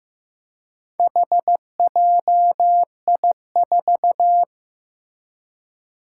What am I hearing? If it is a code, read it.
Morse code, heard as a keyed tone, about 15 wpm, 700 Hz. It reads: HJI4
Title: Four Challenging Letter-Number Combinations 15wpm